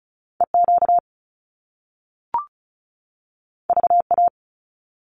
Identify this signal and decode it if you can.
Morse code, heard as a keyed tone, about 36 wpm, 700 Hz. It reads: EQ VA
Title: Top Two Letter Combinations in Words 35wpm